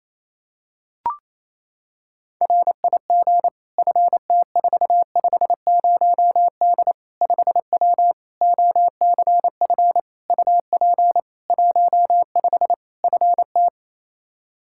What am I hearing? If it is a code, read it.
Morse code, heard as a keyed tone, about 28 wpm, 700 Hz. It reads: RIG FT450D 5W OCF UP 15 FT
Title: K1USN SST QSO Practice 28wpm